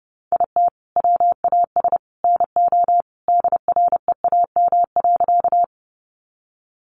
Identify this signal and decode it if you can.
Morse code, heard as a keyed tone, about 28 wpm, 700 Hz. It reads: IT WAS NO DREAM.
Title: Sentences from Top 1500 Words 30wpm